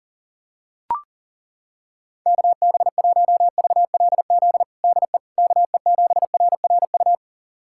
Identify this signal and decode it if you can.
Morse code, heard as a keyed tone, about 40 wpm, 700 Hz. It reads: KB1VLZ DE KE7RRU
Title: USA Callsign DE Callsign 40wpm